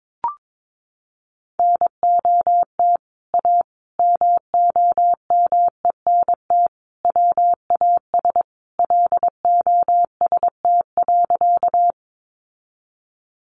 Morse code, heard as a keyed tone, about 22 wpm, 700 Hz. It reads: NOT A MOMENT WAS LOST.
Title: Sentences from Top 1400 Words 22wpm